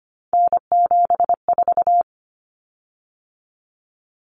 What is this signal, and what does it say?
Morse code, heard as a keyed tone, about 25 wpm, 700 Hz. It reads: N74
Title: Common Cut Numbers - 3-Digit 25wpm